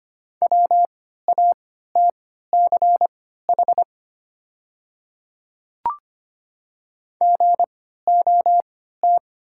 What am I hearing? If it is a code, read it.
Morse code, heard as a keyed tone, about 25 wpm, spaced as for 14 wpm, 690 Hz. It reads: WATCH GOT
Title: 3x Character Spacing - Top 300 Words - 25wpm